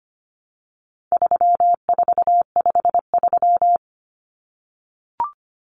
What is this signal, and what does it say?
Morse code, heard as a keyed tone, about 25 wpm, 700 Hz. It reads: 3453
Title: Numbers - 4-Digit 25wpm